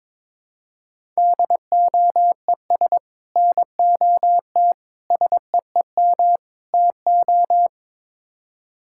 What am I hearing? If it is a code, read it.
Morse code, heard as a keyed tone, about 22 wpm, 700 Hz. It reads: DOES NOT SEEM TO
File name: practice / 22wpm / Build A Phrase - 5-Word Phrases - 22wpm